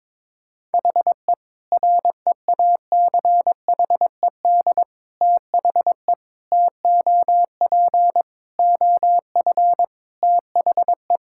Morse code, heard as a keyed tone, about 22 wpm, 700 Hz. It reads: HE REACHED THE TOP OF THE
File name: practice / 22wpm / Sentences with 3-Word Phrases - 22wpm